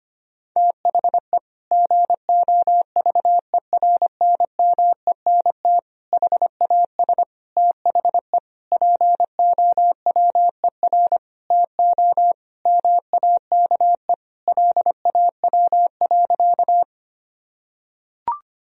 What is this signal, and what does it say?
Morse code, heard as a keyed tone, about 25 wpm, 700 Hz. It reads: THE GOVERNMENT HAS THE POWER TO MAKE LAW.
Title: Sentences from Top 1000 Words - Review Part 3 - 25wpm